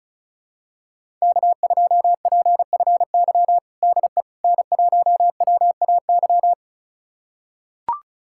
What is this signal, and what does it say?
Morse code, heard as a keyed tone, about 34 wpm, 700 Hz. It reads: K2PFY DE N1WAY
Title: USA Callsign DE Callsign 35wpm